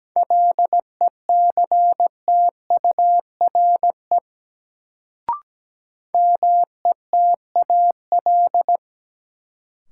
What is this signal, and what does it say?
Morse code, heard as a keyed tone, about 17 wpm, 700 Hz. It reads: LECTURE METAL
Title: Sentences from Top 1600 Words - New Word Drill Alt - 17wpm